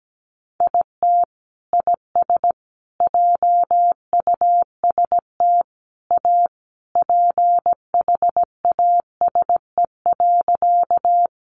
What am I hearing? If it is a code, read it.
Morse code, heard as a keyed tone, about 17 wpm, 700 Hz. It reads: IT IS JUST A PHASE.
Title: Sentences from Top 1200 Words 17wpm